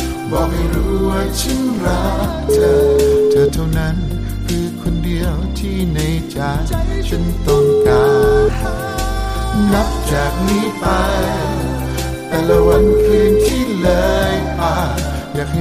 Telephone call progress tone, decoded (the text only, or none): ringback tone